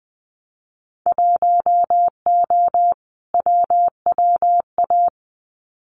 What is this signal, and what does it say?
Morse code, heard as a keyed tone, about 20 wpm, 700 Hz. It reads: 1O WWA
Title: Winter Field Day Exchanges 20wpm